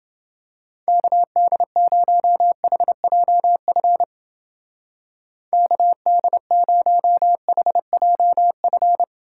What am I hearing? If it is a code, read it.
Morse code, heard as a keyed tone, about 29 wpm, 700 Hz. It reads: KD0HJF KD0HJF